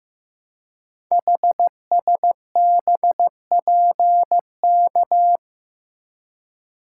Morse code, heard as a keyed tone, about 15 wpm, 700 Hz. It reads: HSBPK
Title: Code Groups - 5 Letters - 15wpm